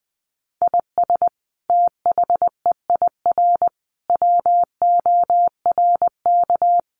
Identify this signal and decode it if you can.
Morse code, heard as a keyed tone, about 20 wpm, 700 Hz. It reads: IS THEIR WORK
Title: Sentences from Top 1500 Words 20wpm